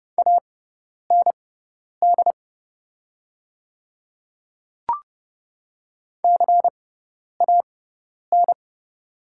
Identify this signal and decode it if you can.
Morse code, heard as a keyed tone, about 30 wpm, spaced as for 10 wpm, 700 Hz. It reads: AND CAN
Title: 6x Character Spacing - Top 200 Words - 30wpm